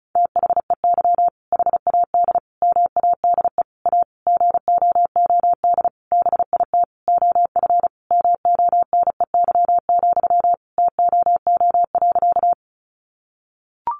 Morse code, heard as a keyed tone, about 34 wpm, 700 Hz. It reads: THEY HAD MADE A GOOD BIT OF MONEY, TOO.
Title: Sentences from Top 900 Words 35wpm